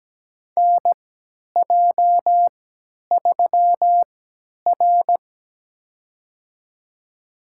Morse code, heard as a keyed tone, about 17 wpm, spaced as for 10 wpm, 700 Hz. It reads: NJ3R
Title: USA Call Signs - 4 Characters - 3x Character Spacing - 17wpm